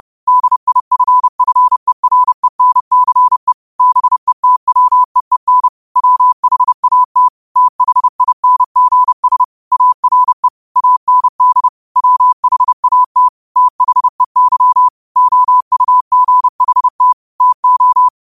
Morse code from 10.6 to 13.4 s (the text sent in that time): AND WHAT